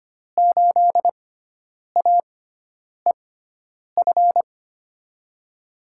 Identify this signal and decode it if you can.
Morse code, heard as a keyed tone, about 25 wpm, spaced as for 9 wpm, 700 Hz. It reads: 8AEF
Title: USA Call Signs - 6x Character Spacing - 25wpm